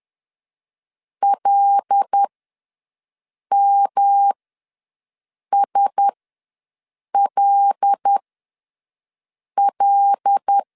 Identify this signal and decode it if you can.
Morse code, heard as a keyed tone, about 10 wpm, spaced as for 5 wpm, 800 Hz. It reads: LMSLL
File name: LEZIONI / 5 LBVC / 5 L ws 10wpm 3636